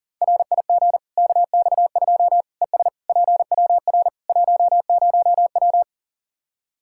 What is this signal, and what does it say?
Morse code, heard as a keyed tone, about 40 wpm, 700 Hz. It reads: RIG KX2 ES PWR 10W
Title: Practice QSO - Exchange 2 - Info - 40wpm